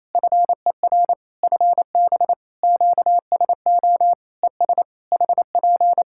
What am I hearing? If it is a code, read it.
Morse code, heard as a keyed tone, about 27 wpm, 700 Hz. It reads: FER FB QSO ES HP
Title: K1USN SST QSO Practice 28wpm